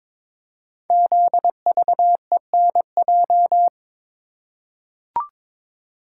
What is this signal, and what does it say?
Morse code, heard as a keyed tone, about 22 wpm, 700 Hz. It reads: ZVENJ
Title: Code Groups - 5 Letters - 22wpm